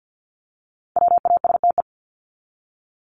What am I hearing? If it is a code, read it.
Morse code, heard as a keyed tone, about 48 wpm, 700 Hz. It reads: WASTE
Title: Top 1751-2000 Words 50wpm